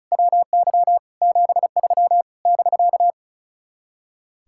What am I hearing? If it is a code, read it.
Morse code, heard as a keyed tone, about 40 wpm, 700 Hz. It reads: WY 73 <BK>
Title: Speed-Racing Common QSO Phrases - 35wpm